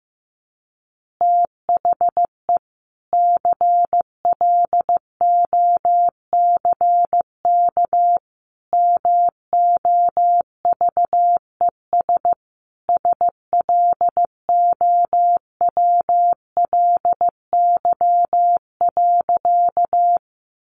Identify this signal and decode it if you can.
Morse code, heard as a keyed tone, about 15 wpm, 700 Hz. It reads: THE CLOCK MOVES SLOWLY.
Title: Easy Phrases 15wpm